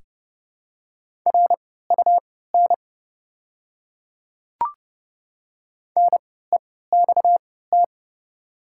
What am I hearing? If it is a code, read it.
Morse code, heard as a keyed tone, about 30 wpm, spaced as for 17 wpm, 700 Hz. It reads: RUN NEXT
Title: 3x Character Spacing - Top 300 Words - 30wpm